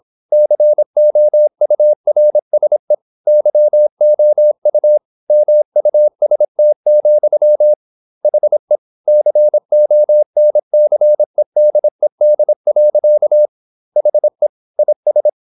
Morse code, heard as a keyed tone, about 26 wpm, 600 Hz. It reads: COURSE YOU MUST, HE CONCEDED. HE IS
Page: Never 26wpm | CW med Gnister